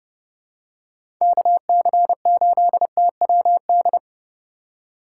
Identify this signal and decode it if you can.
Morse code, heard as a keyed tone, about 29 wpm, 700 Hz. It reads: KC8TWD